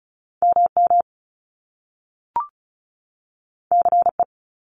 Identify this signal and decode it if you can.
Morse code, heard as a keyed tone, about 36 wpm, 700 Hz. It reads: MM CE
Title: Top Two Letter Combinations in Words 35wpm